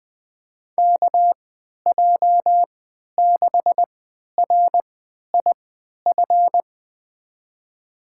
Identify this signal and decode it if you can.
Morse code, heard as a keyed tone, about 20 wpm, spaced as for 11 wpm, 700 Hz. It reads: KJ6RIF
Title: USA Call Signs - 6 Characters - 3x Character Spacing - 20wpm